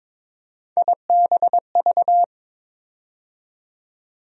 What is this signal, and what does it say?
Morse code, heard as a keyed tone, about 22 wpm, 700 Hz. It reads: IBV
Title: USA Call Sign Postfixes - 3 Letters 22wpm